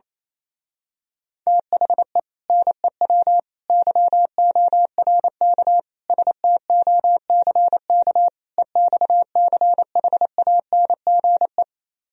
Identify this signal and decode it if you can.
Morse code, heard as a keyed tone, about 27 wpm, 700 Hz. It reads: THE NEW YORK STOCK EXCHANGE
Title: Speed-Racing - Sets of 5 Words 28wpm